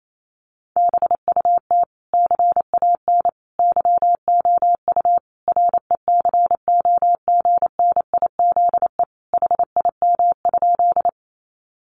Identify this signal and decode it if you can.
Morse code, heard as a keyed tone, about 28 wpm, 700 Hz. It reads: BUT CAN YOU RECOGNIZE HIM?
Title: Sentences from Top 1400 Words 28wpm